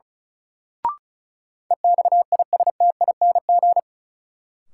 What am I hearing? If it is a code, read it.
Morse code, heard as a keyed tone, about 35 wpm, 700 Hz. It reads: EXISTING
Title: Sentences from Top 1500 Words - New Word Drill Alt - 35wpm